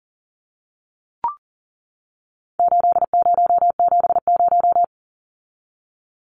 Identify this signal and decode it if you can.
Morse code, heard as a keyed tone, about 40 wpm, 700 Hz. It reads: 8070
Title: Numbers - 4-Digit 40wpm